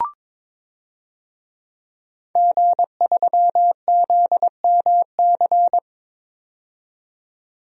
Morse code, heard as a keyed tone, about 22 wpm, 700 Hz. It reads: G3ZMC
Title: DXCC-100 Call Signs - 5 Characters - 22wpm